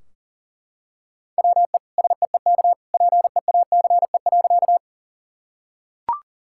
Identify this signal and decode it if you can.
Morse code, heard as a keyed tone, about 37 wpm, 700 Hz. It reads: WE SEEK PEACE.
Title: Sentences from Top 1600 Words 40wpm